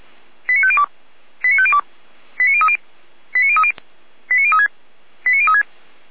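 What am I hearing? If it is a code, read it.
Five-tone selective call (ZVEI), heard as quick runs of five tones, tones of about 70 ms. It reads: 89502 89502 89030 89030 89036 89036
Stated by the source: ZVEI five-tone call 89042 received on SDR.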